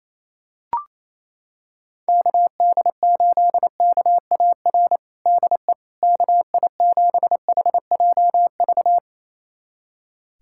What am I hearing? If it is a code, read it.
Morse code, heard as a keyed tone, about 28 wpm, 700 Hz. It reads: KD8KAR DE KI7HJV
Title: USA Callsign DE Callsign - 6 Characters - 28wpm